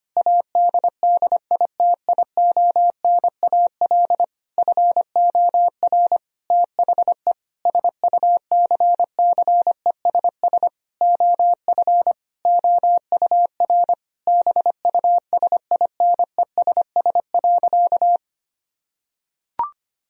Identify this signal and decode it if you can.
Morse code, heard as a keyed tone, about 25 wpm, 700 Hz. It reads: ADDITIONAL FOR THE SUCCESS OF OUR BUSINESS.
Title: Sentences from Top 1000 Words - Review Part 4 - 25wpm